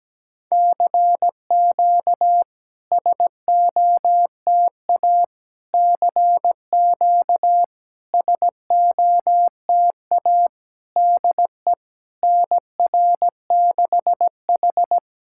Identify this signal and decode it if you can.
Morse code, heard as a keyed tone, about 17 wpm, 690 Hz. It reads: CQ SOTA CQ SOTA DE NR6H
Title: SOTA QSO Practice 17wpm